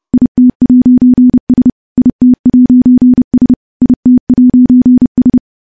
Morse code, heard as a keyed tone, about 31 wpm, 250 Hz. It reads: IT'S IT'S IT'S
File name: morse / morse-training-30WPM